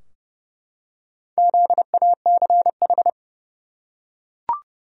Morse code, heard as a keyed tone, about 32 wpm, 700 Hz. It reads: ZACH
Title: USA Common Names - Max 4 Letters - 30wpm